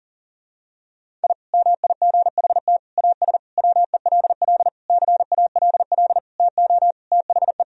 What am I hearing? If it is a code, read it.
Morse code, heard as a keyed tone, about 40 wpm, 700 Hz. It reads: I MIGHT AS WELL CALL TO THE
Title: Sentences from Top 1000 Words - Review Part 2 - 40wpm